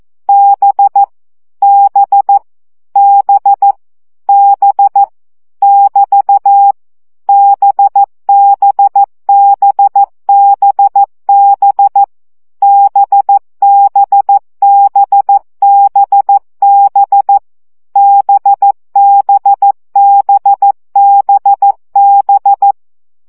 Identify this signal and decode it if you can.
Morse code, heard as a keyed tone, about 14 wpm, 800 Hz. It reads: B B B B <BT> BBBBB BBBBB BBBBB
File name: CW gyakorlas b 12 baud